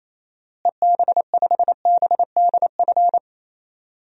Morse code, heard as a keyed tone, about 28 wpm, 700 Hz. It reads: EB5BDF
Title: DXCC-100 Call Signs - 6 Characters - Repeated 3x - 28wpm